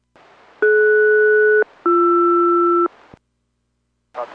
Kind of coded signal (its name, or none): aircraft SELCAL